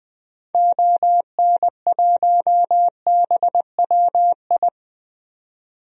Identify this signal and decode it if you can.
Morse code, heard as a keyed tone, about 20 wpm, 700 Hz. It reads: ON1BWI
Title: DXCC-100 Call Signs - 6 Characters - 20wpm